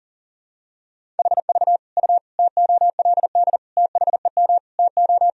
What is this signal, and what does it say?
Morse code, heard as a keyed tone, about 40 wpm, 700 Hz. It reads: HV U TOLD THEM TO
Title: Sentences from Top 500 Words and Abbreviations 40wpm